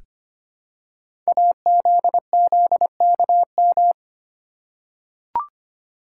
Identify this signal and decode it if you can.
Morse code, heard as a keyed tone, about 25 wpm, 700 Hz. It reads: AZZKM
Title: Code Groups - 5 Letters - 25wpm